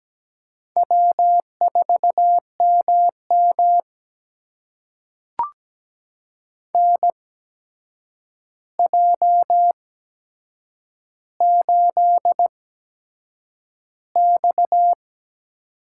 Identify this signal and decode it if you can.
Morse code, heard as a keyed tone, about 17 wpm, spaced as for 7 wpm, 700 Hz. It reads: W4MM NJ8X
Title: USA Call Signs - 4 Characters - 8x Character Spacing - 17wpm